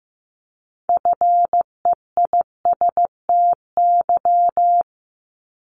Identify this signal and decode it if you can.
Morse code, heard as a keyed tone, about 15 wpm, 700 Hz. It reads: FEISTY
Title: Multi-letter Phonemes 15wpm